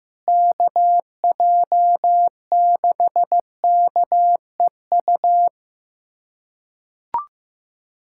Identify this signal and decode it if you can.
Morse code, heard as a keyed tone, about 15 wpm, 700 Hz. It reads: KJ6KEU
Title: USA Callsign DE Callsign - 6 Characters - 15wpm